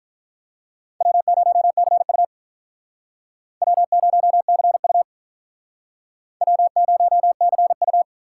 Morse code, heard as a keyed tone, about 44 wpm, 700 Hz. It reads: W0CU W0CU W0CU